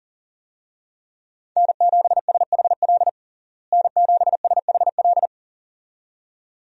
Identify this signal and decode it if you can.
Morse code, heard as a keyed tone, about 40 wpm, 700 Hz. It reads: N7SHL N7SHL
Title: USA Call Signs - 5 Characters - Repeated 2x - 40wpm